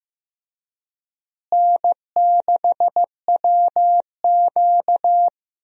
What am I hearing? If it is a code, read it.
Morse code, heard as a keyed tone, about 15 wpm, 700 Hz. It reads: N6WQ
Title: USA Call Signs - 4 Characters - Repeated 3x - 15wpm